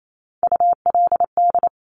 Morse code, heard as a keyed tone, about 28 wpm, 690 Hz. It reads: ULD
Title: Top Three Letter Combinations in Words 28wpm